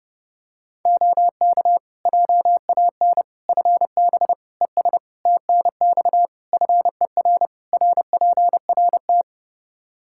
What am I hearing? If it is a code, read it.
Morse code, heard as a keyed tone, about 30 wpm, 700 Hz. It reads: OK JAN FB ES TNX FER RPRT
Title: K1USN SST QSO Practice 30wpm